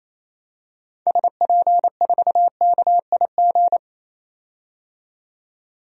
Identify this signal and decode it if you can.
Morse code, heard as a keyed tone, about 28 wpm, 700 Hz. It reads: SP4KIG
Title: DXCC-100 Call Signs - 6 Characters - 28wpm